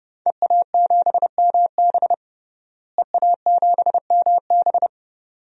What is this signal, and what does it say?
Morse code, heard as a keyed tone, about 32 wpm, 700 Hz. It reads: EA7MB EA7MB
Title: DXCC-100 Call Signs - 5 Characters - Repeated 2x - 30wpm